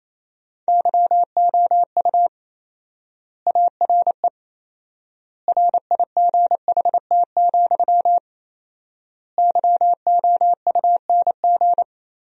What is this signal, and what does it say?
Morse code, heard as a keyed tone, about 28 wpm, 700 Hz. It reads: YOU ARE RIGHT, YOUNG
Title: 4x Word-Spacing - Sentences from Top 400 Words - 28wpm